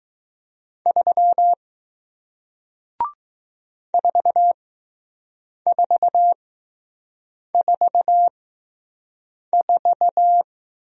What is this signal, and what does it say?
Morse code, heard as a keyed tone, about 20 wpm, spaced as for 6 wpm, 700 Hz. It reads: 3 4444